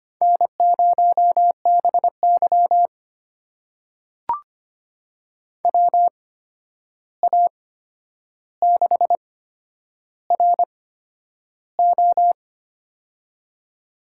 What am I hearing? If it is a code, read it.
Morse code, heard as a keyed tone, about 25 wpm, spaced as for 10 wpm, 700 Hz. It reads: N0BY WA6RO
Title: USA Call Signs - 8x Character Spacing - 25wpm